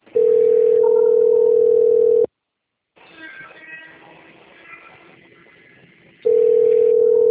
Ringback tone (North America), 440 + 480 Hz.
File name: out- - - - - .15525